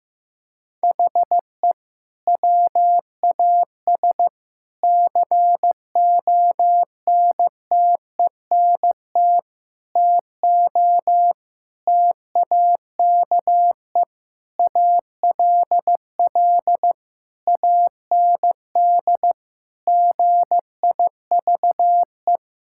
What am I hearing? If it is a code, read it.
Morse code, heard as a keyed tone, about 14 wpm, 700 Hz. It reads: HE WAS CONTENT TO TAKE ALL AND GIVE
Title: Sentences from Top 1000 Words - Review Part 6 - 15wpm